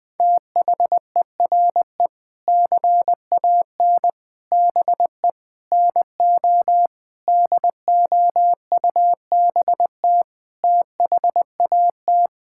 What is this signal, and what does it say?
Morse code, heard as a keyed tone, about 21 wpm, 700 Hz. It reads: THERE CAN BE NO DOUBT THAT
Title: Sentences with 7-Word Phrases - Spoken Beginning - 20wpm